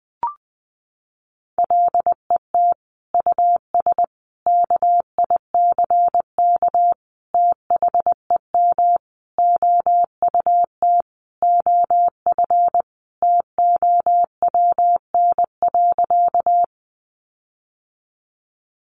Morse code, heard as a keyed tone, about 20 wpm, 700 Hz. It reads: LET US KICK THEM OUT OF TOWN.
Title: Sentences from Top 1300 Words 20wpm